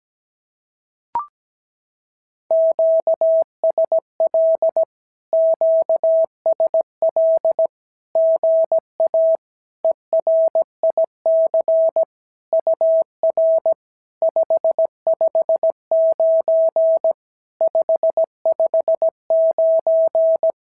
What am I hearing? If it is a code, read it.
Morse code, heard as a keyed tone, about 17 wpm, 650 Hz. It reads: QSL QSL GA ERIC UR 559 559
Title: SOTA QSO Practice 17wpm